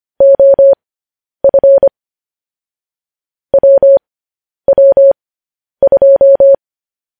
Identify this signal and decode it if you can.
Morse code, heard as a keyed tone, about 25 wpm, spaced as for 10 wpm, 550 Hz. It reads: OF WW2
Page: News Headlines in Morse Code at 10 WPM